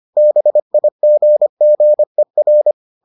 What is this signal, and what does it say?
Morse code, heard as a keyed tone, about 25 wpm, 600 Hz. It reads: BIGGER